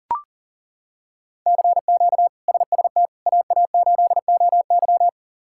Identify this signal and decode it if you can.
Morse code, heard as a keyed tone, about 40 wpm, 700 Hz. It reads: CQ SST AA8OY
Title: K1USN SST QSO Practice - 40wpm